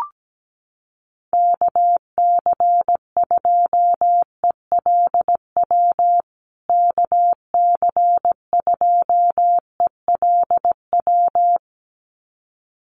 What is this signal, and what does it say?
Morse code, heard as a keyed tone, about 17 wpm, 700 Hz. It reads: KC2ELW KC2ELW
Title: USA Call Signs Repeated 2x 17wpm